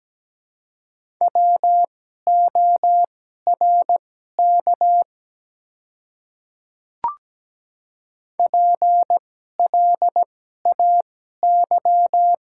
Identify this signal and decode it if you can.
Morse code, heard as a keyed tone, about 17 wpm, spaced as for 12 wpm, 700 Hz. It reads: WORK PLAY
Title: 2x Character Spacing - Top 200 Words - 17wpm